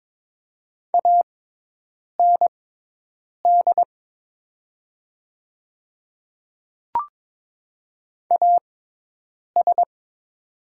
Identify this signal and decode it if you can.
Morse code, heard as a keyed tone, about 22 wpm, spaced as for 8 wpm, 700 Hz. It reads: AND AS
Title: 6x Character Spacing - Top 100 Words - 22wpm